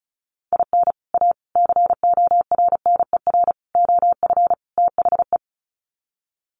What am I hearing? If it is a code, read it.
Morse code, heard as a keyed tone, about 34 wpm, 700 Hz. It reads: IN A CORNER OF THE
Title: Sets of 5 Words Encore2 35wpm